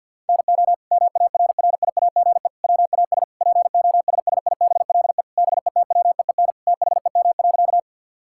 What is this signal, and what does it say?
Morse code, heard as a keyed tone, about 50 wpm, 700 Hz. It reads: NO MARRIAGE WAS POSSIBLE BETWEEN THEM.
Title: Sentences from Top 1000 Words - Review Part 5 - 50wpm